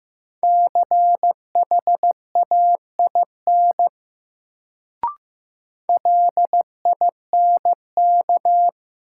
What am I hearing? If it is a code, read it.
Morse code, heard as a keyed tone, about 15 wpm, 700 Hz. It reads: CHAIN LINK
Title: Sentences from Top 1500 Words - New Word Drill Alt - 15wpm